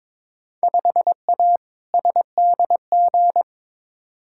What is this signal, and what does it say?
Morse code, heard as a keyed tone, about 22 wpm, 700 Hz. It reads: 5A SDG
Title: Field Day Exchanges Encore - 22wpm